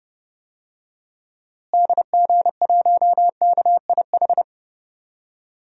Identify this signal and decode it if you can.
Morse code, heard as a keyed tone, about 30 wpm, 700 Hz. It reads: DG1KIH
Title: DXCC-100 Call Signs - 6 Characters - Repeated 3x - 30wpm